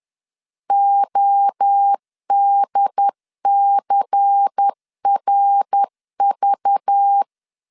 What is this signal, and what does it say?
Morse code, heard as a keyed tone, about 10 wpm, 800 Hz. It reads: ODCRV
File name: LEZIONI / 7 GF9Z / 7 GF 10wpm 2514